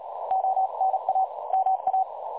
Morse code, heard as a keyed tone, about 36 wpm, 760 Hz. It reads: ONA MA
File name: dlazaawansowanych / nagraniazpasma / websdr recording 3565.0kHz